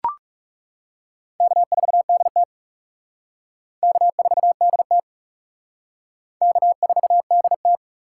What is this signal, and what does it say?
Morse code, heard as a keyed tone, about 36 wpm, 700 Hz. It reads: K4DT K4DT K4DT
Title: Speed Racing - USA 1x2 Call Signs 30wpm